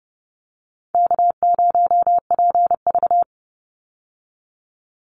Morse code, heard as a keyed tone, about 30 wpm, 700 Hz. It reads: K0PV
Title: Easy Call Signs Repeated 3x - 30wpm